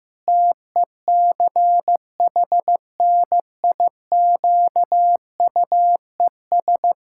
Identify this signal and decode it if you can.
Morse code, heard as a keyed tone, about 15 wpm, 700 Hz. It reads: TECHNIQUES
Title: Pangram Challenge v2 - 15wpm